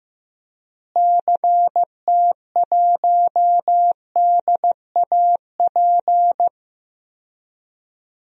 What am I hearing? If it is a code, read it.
Morse code, heard as a keyed tone, about 15 wpm, 700 Hz. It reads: CT1DAP
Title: DXCC-100 Call Signs - 6 Characters - Repeated 3x - 15wpm